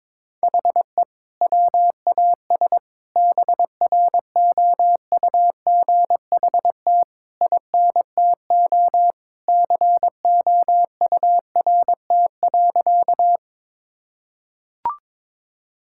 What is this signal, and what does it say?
Morse code, heard as a keyed tone, about 21 wpm, 700 Hz. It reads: HE WAS BROUGHT INTO COURT.
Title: Sentences from Top 1600 Words 22wpm